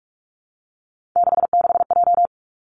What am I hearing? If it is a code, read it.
Morse code, heard as a keyed tone, about 46 wpm, 700 Hz. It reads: 66J
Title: Three Challenging Letter-Number Combinations 45wpm